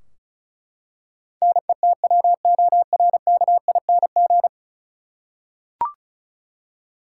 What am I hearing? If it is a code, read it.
Morse code, heard as a keyed tone, about 36 wpm, 700 Hz. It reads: NETWORKING
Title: Practice QSO - Exchange 3 - More Info - 35wpm